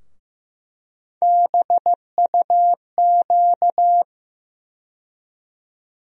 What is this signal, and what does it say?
Morse code, heard as a keyed tone, about 15 wpm, 700 Hz. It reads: BUQ
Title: USA Call Sign Postfixes - 3 Letters 15wpm